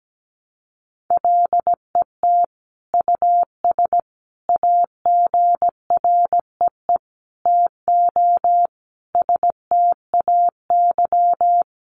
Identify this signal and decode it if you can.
Morse code, heard as a keyed tone, about 17 wpm, 700 Hz. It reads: LET US AGREE TO STAY
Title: Sentences from Top 1100 Words 17wpm